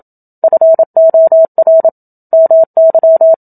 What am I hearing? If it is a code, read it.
Morse code, heard as a keyed tone, about 27 wpm, 650 Hz. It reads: FOR MY
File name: Book Two / 01-Chapter 1 - 27 WPM